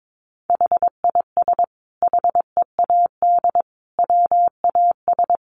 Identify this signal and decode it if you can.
Morse code, heard as a keyed tone, about 22 wpm, 700 Hz. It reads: HIS HEAD WAS
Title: Sentences from Top 1100 Words 22wpm